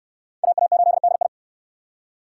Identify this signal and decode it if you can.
Morse code, heard as a keyed tone, about 68 wpm, 700 Hz. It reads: AA8DI